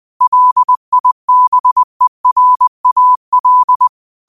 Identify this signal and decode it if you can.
Morse code, heard as a keyed tone, about 20 wpm, 1000 Hz. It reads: LIBERAL